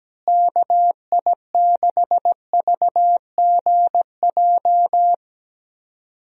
Morse code, heard as a keyed tone, about 17 wpm, 700 Hz. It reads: KI6VGJ
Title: USA Call Signs - 6 Characters - Repeated 2x - 17wpm